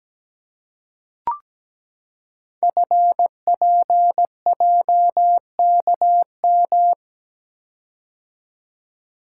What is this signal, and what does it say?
Morse code, heard as a keyed tone, about 17 wpm, 700 Hz. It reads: FPJKM
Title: Code Groups - 5 Letter:Numbers - 17wpm